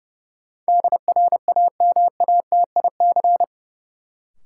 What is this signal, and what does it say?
Morse code, heard as a keyed tone, about 32 wpm, 700 Hz. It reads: DRAMATIC
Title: Sentences from Top 1300 Words - New Word Drill Alt - 30wpm